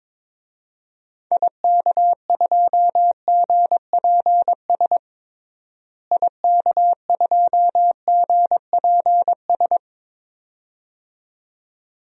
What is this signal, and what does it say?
Morse code, heard as a keyed tone, about 22 wpm, 700 Hz. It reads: IK2GPS IK2GPS
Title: DXCC-100 Call Signs - 6 Characters - Repeated 2x - 22wpm